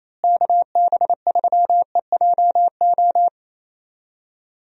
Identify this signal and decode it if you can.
Morse code, heard as a keyed tone, about 27 wpm, 700 Hz. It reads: KB3EJO
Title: USA Call Signs - 6 Characters - Repeated 3x - 28wpm